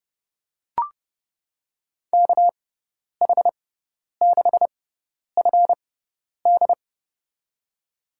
Morse code, heard as a keyed tone, about 30 wpm, spaced as for 10 wpm, 700 Hz. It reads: KH6FD
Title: USA Call Signs - 5 Characters - 6x Character Spacing - 30wpm